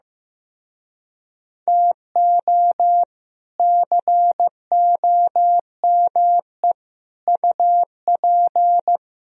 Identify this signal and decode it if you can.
Morse code, heard as a keyed tone, about 15 wpm, 700 Hz. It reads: TO COME UP
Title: Speed-Racing - Sets of 5 Words 15wpm